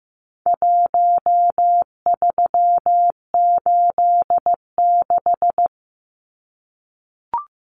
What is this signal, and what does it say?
Morse code, heard as a keyed tone, about 15 wpm, 700 Hz. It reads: 1386
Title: Numbers - Contest Style - 4-Digit 15wpm